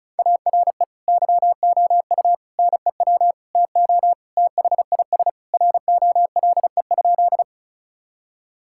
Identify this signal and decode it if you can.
Morse code, heard as a keyed tone, about 36 wpm, 700 Hz. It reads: ARE YOU NEW TO THIS ROLE?
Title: Sentences from Top 1000 Words - Review Part 6 - 35wpm